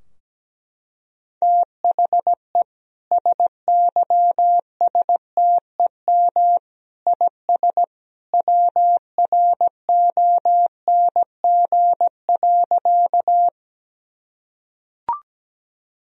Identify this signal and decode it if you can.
Morse code, heard as a keyed tone, about 17 wpm, 700 Hz. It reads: THE SYSTEM IS WRONG.
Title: Sentences from Top 1000 Words - Review Part 3 - 17wpm